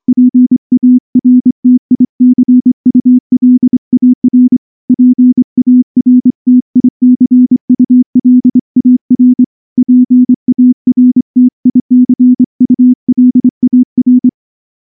Morse code, heard as a keyed tone, about 24 wpm, 250 Hz. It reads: PARTICULAR PARTICULAR PARTICULAR
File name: morse / morse-training-25WPM